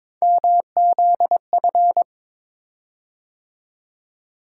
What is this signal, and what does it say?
Morse code, heard as a keyed tone, about 22 wpm, 700 Hz. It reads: MZF
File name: practice / 22wpm / Three Letter-Number Combinations - 22wpm